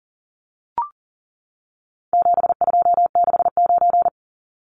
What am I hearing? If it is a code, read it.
Morse code, heard as a keyed tone, about 40 wpm, 700 Hz. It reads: 7269
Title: Numbers - 4-Digit 40wpm